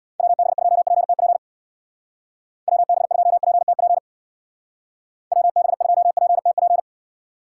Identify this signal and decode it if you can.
Morse code, heard as a keyed tone, about 59 wpm, 700 Hz. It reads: WB2PTP WB2PTP WB2PTP
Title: Speed Racing - USA 2x3 Call Signs 50wpm